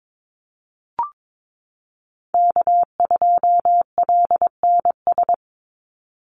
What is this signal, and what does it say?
Morse code, heard as a keyed tone, about 22 wpm, 700 Hz. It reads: K2LNS
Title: Call Signs - 22wpm